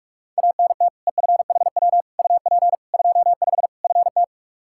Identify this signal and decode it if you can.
Morse code, heard as a keyed tone, about 46 wpm, 700 Hz. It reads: ANT EFHW UP 25 FT
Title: K1USN SST QSO Practice 45wpm